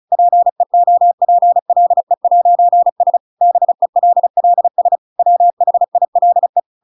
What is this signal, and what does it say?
Morse code, heard as a keyed tone, about 34 wpm, 700 Hz. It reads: PEOPLE'S BELLS WHILE